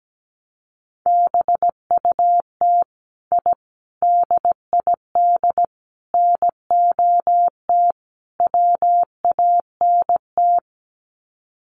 Morse code, heard as a keyed tone, about 17 wpm, 700 Hz. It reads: BUT I DID NOT WANT
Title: Sets of 5 Words Encore 17wpm